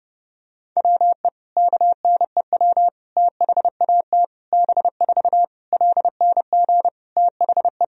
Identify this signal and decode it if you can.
Morse code, heard as a keyed tone, about 30 wpm, 700 Hz. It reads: WE KNEW THAT B4 LNG THE
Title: Sentences from Top 400 Words and Abbreviations 30wpm